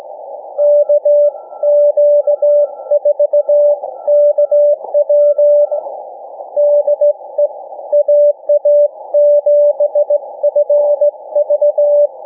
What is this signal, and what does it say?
Morse code, heard as a keyed tone, about 18 wpm, 590 Hz. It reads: KQ4KW DE AA7FV
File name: cw / ssn 40M